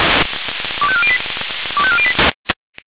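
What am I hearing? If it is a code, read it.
Five-tone selective call (ZVEI), heard as quick runs of five tones, tones of about 70 ms. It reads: 25408 25408
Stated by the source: ZVEI five-tone selective call received by SDR.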